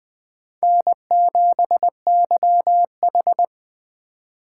Morse code, heard as a keyed tone, about 20 wpm, 700 Hz. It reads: N7YH
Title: Speed Racing - USA Call Signs 17wpm